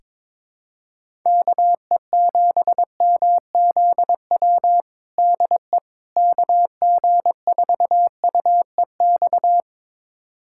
Morse code, heard as a keyed tone, about 22 wpm, 700 Hz. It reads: KE7MZW DE KG4UEX
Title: USA Callsign DE Callsign - 6 Characters - 22wpm